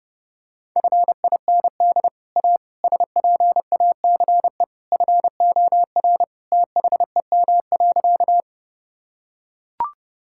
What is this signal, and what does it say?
Morse code, heard as a keyed tone, about 32 wpm, 700 Hz. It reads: FIND A SPACE FOR THEM.
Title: Sentences from Top 1000 Words - Review Part 5 - 30wpm